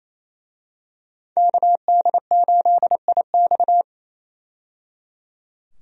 Morse code, heard as a keyed tone, about 28 wpm, 700 Hz. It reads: KD8IX
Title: USA Call Signs - 5 Characters - 28wpm